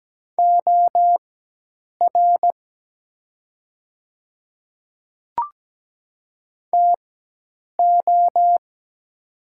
Morse code, heard as a keyed tone, about 17 wpm, spaced as for 8 wpm, 700 Hz. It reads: OR TO
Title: 4x Character Spacing - Top 100 Words - 17wpm